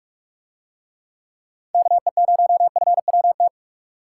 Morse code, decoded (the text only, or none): KE0FWT